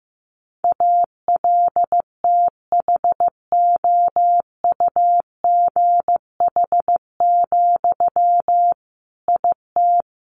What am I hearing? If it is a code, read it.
Morse code, heard as a keyed tone, about 15 wpm, 700 Hz. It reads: ALTHOUGH, IT
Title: Sentences from Top 600 Words 15wpm